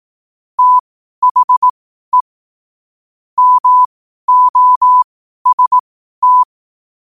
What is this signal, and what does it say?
Morse code, heard as a keyed tone, about 18 wpm, spaced as for 12 wpm, 1000 Hz. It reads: THE MOST
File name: qotdfiles / 12wpm